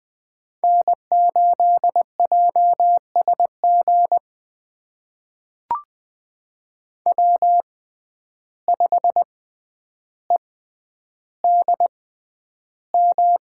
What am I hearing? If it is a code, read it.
Morse code, heard as a keyed tone, about 21 wpm, spaced as for 10 wpm, 700 Hz. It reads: N8JSG W5EDM
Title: USA Call Signs - 5 Characters - 6x Character Spacing - 20wpm